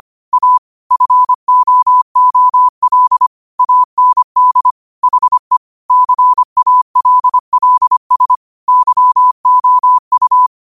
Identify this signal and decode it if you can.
Morse code, heard as a keyed tone, about 25 wpm, 1000 Hz. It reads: A FOOL AND HE CALLS YOU
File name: qotdfiles / 25wpm